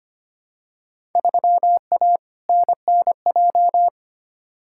Morse code, heard as a keyed tone, about 25 wpm, 700 Hz. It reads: 3A NNJ
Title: Field Day Exchanges - 25wpm